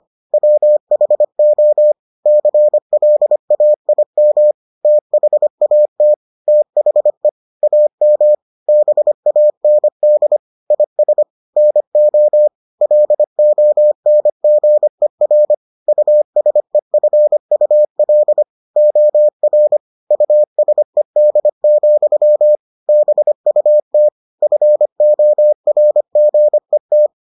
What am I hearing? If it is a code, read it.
Morse code, heard as a keyed tone, about 25 wpm, 600 Hz. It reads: WHO CLAIM THAT THE AM BAND IS NO LONGER USEFUL OR USED, BUT FORGET